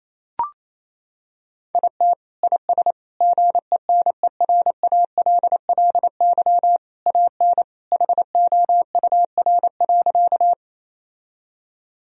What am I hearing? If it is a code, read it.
Morse code, heard as a keyed tone, about 28 wpm, 700 Hz. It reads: IT IS GENERALLY AN HOUR.
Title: Sentences from Top 1000 Words - Review Part 4 - 28wpm